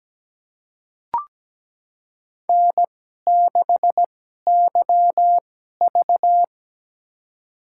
Morse code, heard as a keyed tone, about 17 wpm, spaced as for 12 wpm, 700 Hz. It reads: N6YV
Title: USA Call Signs - 4 Characters - 2x Character Spacing - 17wpm